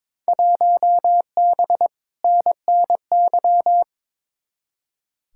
Morse code, heard as a keyed tone, about 22 wpm, 700 Hz. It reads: 1B NNY
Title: Field Day Exchanges Encore - 22wpm